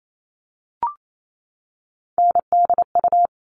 Morse code, heard as a keyed tone, about 28 wpm, 700 Hz. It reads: NDU
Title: All Cut Numbers - 3-Digit 28wpm